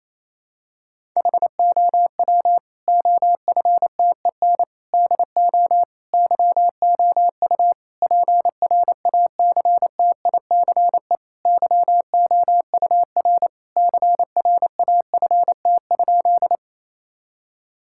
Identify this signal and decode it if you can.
Morse code, heard as a keyed tone, about 27 wpm, 700 Hz. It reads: HOW OFTEN DO YOU PRACTICE YOUR CRAFT?
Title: Sentences from Top 1000 Words - Review Part 6 - 28wpm